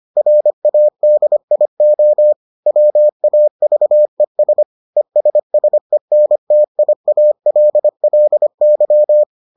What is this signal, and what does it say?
Morse code, heard as a keyed tone, about 25 wpm, 600 Hz. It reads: RADIO WAVES ESSENTIALLY